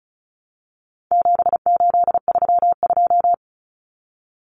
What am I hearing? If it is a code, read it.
Morse code, heard as a keyed tone, about 34 wpm, 700 Hz. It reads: 7832
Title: Common Cut Numbers - 4-Digit 35wpm